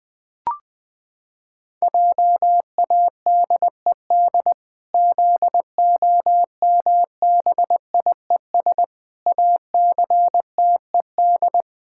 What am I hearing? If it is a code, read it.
Morse code, heard as a keyed tone, about 20 wpm, 700 Hz. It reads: JADED ZOMBIES ACTED
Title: Pangram Challenge v2 - 20wpm